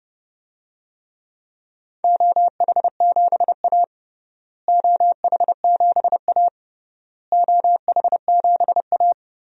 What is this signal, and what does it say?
Morse code, heard as a keyed tone, about 32 wpm, 700 Hz. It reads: OH7A OH7A OH7A
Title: DXCC-100 Call Signs - 4 Characters - Repeated 3x - 30wpm